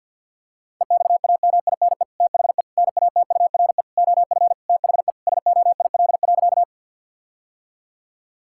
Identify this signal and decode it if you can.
Morse code, heard as a keyed tone, about 50 wpm, 700 Hz. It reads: EXAMINE THE NATURE OF THE SOIL.
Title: Sentences from Top 1600 Words 50wpm